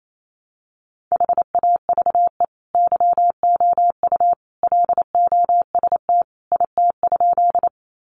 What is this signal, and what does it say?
Morse code, heard as a keyed tone, about 28 wpm, 700 Hz. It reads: HAVE YOU LOST IT?
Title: Sentences from Top 1400 Words 28wpm